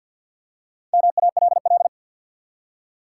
Morse code, heard as a keyed tone, about 54 wpm, 700 Hz. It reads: MALL
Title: Four Letter Words - 50wpm